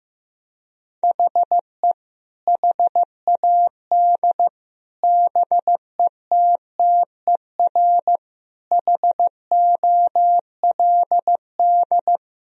Morse code, heard as a keyed tone, about 15 wpm, 700 Hz. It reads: HE HAD BETTER HOLD
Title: Sentences from Top 1000 Words - Review Part 2 - 15wpm